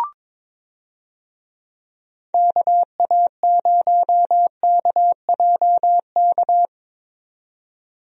Morse code, heard as a keyed tone, about 22 wpm, 700 Hz. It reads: KA0KJK
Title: USA Call Signs - 6 Characters - 22wpm